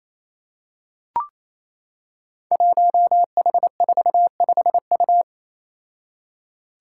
Morse code, heard as a keyed tone, about 27 wpm, 700 Hz. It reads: 1H45U
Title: Five Challenging Letter-Number Combinations 28wpm